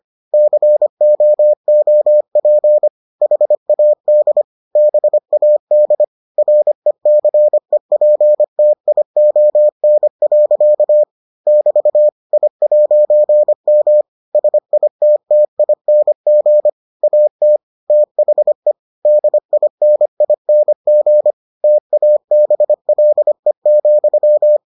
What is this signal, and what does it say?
Morse code, heard as a keyed tone, about 25 wpm, 600 Hz. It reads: COOP HAD BAD RECEPTION. <BT> I'M SITTING AT THE DINING TABLE,